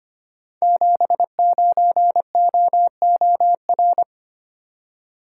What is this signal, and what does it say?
Morse code, heard as a keyed tone, about 25 wpm, 700 Hz. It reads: 79OOR
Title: Code Groups - 5 Letter:Numbers - 25wpm